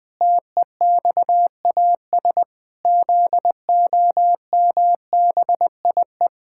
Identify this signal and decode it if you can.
Morse code, heard as a keyed tone, about 20 wpm, 700 Hz. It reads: TEXAS ZOMBIE
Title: Pangram Challenge v2 - 20wpm